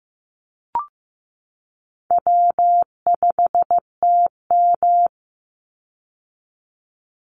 Morse code, heard as a keyed tone, about 15 wpm, 700 Hz. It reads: W5TM
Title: Call Signs Repeated 2x - 15wpm